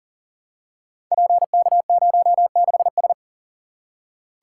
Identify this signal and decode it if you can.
Morse code, heard as a keyed tone, about 38 wpm, 700 Hz. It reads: PK06S
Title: Code Groups - 5 Letter:Numbers - 40wpm